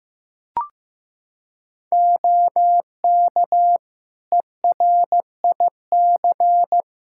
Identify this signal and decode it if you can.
Morse code, heard as a keyed tone, about 15 wpm, 700 Hz. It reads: OK ERIC
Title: Practice QSO - Exchange 3 - More Info - 15wpm